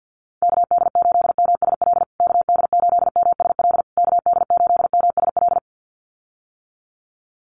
Morse code, heard as a keyed tone, about 50 wpm, 700 Hz. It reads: KD8MSL KD8MSL KD8MSL
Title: USA Call Signs Repeated 3x 50wpm